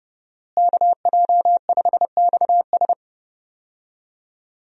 Morse code, heard as a keyed tone, about 32 wpm, 700 Hz. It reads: KJ5XS
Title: USA Call Signs - 5 Characters - Repeated 2x - 30wpm